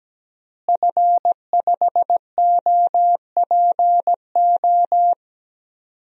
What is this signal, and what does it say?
Morse code, heard as a keyed tone, about 17 wpm, 700 Hz. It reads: F5OPO
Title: DXCC-100 Call Signs - 5 Characters - Repeated 3x - 17wpm